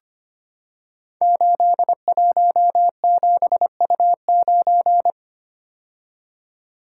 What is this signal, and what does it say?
Morse code, heard as a keyed tone, about 25 wpm, 700 Hz. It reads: 817U9
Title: Code Groups - 5 Letter:Numbers - 25wpm